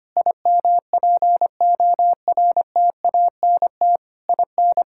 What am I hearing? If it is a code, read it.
Morse code, heard as a keyed tone, about 25 wpm, 700 Hz. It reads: IMPORTANT IN
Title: Sentences from Top 1000 Words - Review Part 5 - 25wpm